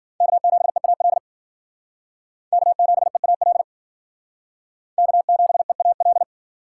Morse code, heard as a keyed tone, about 51 wpm, 700 Hz. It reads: K7EAL K7EAL K7EAL